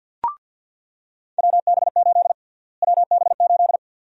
Morse code, heard as a keyed tone, about 47 wpm, 700 Hz. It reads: WB8 WB8
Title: USA Call Sign Prefixes - Repeated 2x - 50wpm